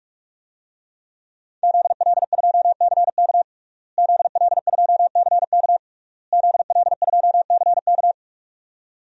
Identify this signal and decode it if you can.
Morse code, heard as a keyed tone, about 41 wpm, 700 Hz. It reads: ZL2CK ZL2CK ZL2CK
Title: DXCC-100 Call Signs - 5 Characters - Repeated 3x - 45wpm